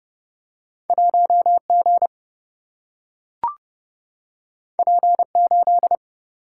Morse code, heard as a keyed tone, about 31 wpm, 700 Hz. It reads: 1 G P 8
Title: Single-Letter Single-Number 30wpm